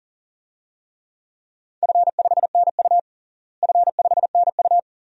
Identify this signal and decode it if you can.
Morse code, heard as a keyed tone, about 40 wpm, 700 Hz. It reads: F5NU F5NU
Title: DXCC-100 Call Signs - 4 Characters - Repeated 2x - 40wpm